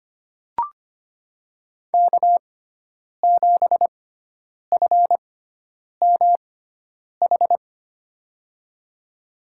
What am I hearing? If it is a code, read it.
Morse code, heard as a keyed tone, about 25 wpm, spaced as for 9 wpm, 700 Hz. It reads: K7FMH
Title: USA Call Signs - 5 Characters - 6x Character Spacing - 25wpm